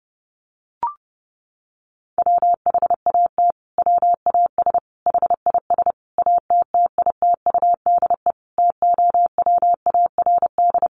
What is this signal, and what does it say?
Morse code, heard as a keyed tone, about 28 wpm, 700 Hz. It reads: WHAT WAS HIS ATTITUDE TOWARD
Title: Sentences from Top 1300 Words 30wpm